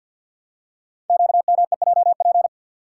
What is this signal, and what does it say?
Morse code, heard as a keyed tone, about 50 wpm, 700 Hz. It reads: QMEJP
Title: Code Groups - 5 Letter:Numbers - 50wpm